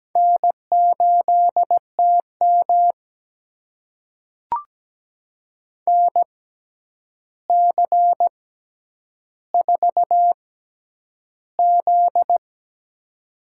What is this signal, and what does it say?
Morse code, heard as a keyed tone, about 17 wpm, spaced as for 9 wpm, 700 Hz. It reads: N8TM NC4Z
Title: USA Call Signs - 4 Characters - 6x Character Spacing - 17wpm